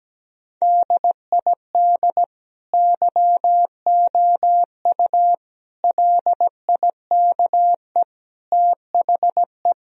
Morse code, heard as a keyed tone, about 17 wpm, 700 Hz. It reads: DID YOU LIKE THE
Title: Sentences from Top 1000 Words - Review Part 4 - 17wpm